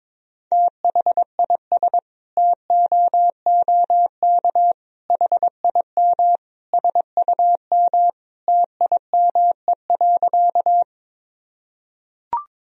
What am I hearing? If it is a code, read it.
Morse code, heard as a keyed tone, about 22 wpm, 700 Hz. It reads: THIS TOOK HIM SUM TIME.
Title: Sentences from Top 300 Words and Abbreviations 22wpm